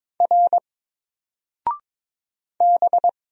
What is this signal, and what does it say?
Morse code, heard as a keyed tone, about 22 wpm, 700 Hz. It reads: R B
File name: Single-Letter-Number-22wpm